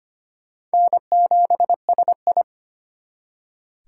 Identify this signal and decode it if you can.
Morse code, heard as a keyed tone, about 32 wpm, 700 Hz. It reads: N7SI